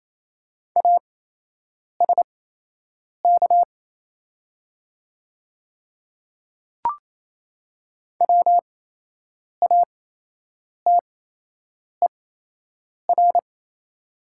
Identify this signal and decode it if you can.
Morse code, heard as a keyed tone, about 28 wpm, spaced as for 8 wpm, 700 Hz. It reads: ASK WATER
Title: 8x Character Spacing - Top 200 Words - 28wpm